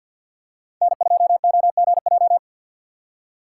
Morse code, heard as a keyed tone, about 50 wpm, 700 Hz. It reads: N2OGJ
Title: USA Call Signs - 2x Character Spacing - 50wpm